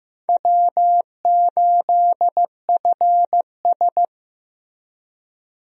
Morse code, heard as a keyed tone, about 15 wpm, 700 Hz. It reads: W8FS
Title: USA Call Signs - 4 Characters - Repeated 3x - 15wpm